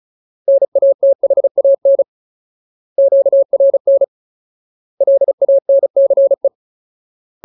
Morse code, heard as a keyed tone, about 35 wpm, 550 Hz. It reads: NATHAN QRN LANCE
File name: Ultimate Word List / 35WPM Ultimate